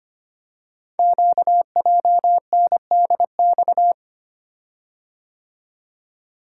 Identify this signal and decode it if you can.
Morse code, heard as a keyed tone, about 25 wpm, 700 Hz. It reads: QJNDX
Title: Code Groups - 5 Letters - 25wpm